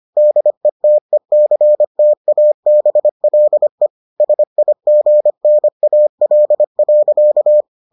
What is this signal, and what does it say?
Morse code, heard as a keyed tone, about 25 wpm, 600 Hz. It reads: DETECTABLE SIGNAL.